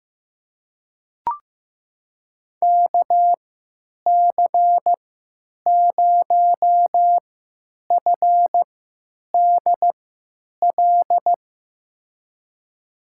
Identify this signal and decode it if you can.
Morse code, heard as a keyed tone, about 15 wpm, spaced as for 9 wpm, 700 Hz. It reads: KC0FDL
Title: USA Call Signs - 6 Characters - 3x Character Spacing - 15wpm